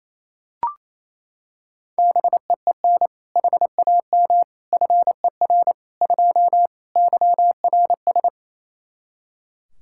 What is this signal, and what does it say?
Morse code, heard as a keyed tone, about 27 wpm, 700 Hz. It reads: BEEN HAM FER 2 YRS
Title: Practice QSO - Exchange 3 - More Info - 28wpm